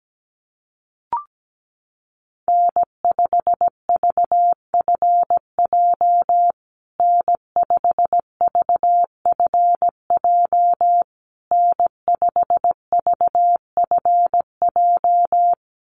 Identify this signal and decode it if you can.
Morse code, heard as a keyed tone, about 17 wpm, 700 Hz. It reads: N5VFJ N5VFJ N5VFJ
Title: USA Call Signs Repeated 3x 17wpm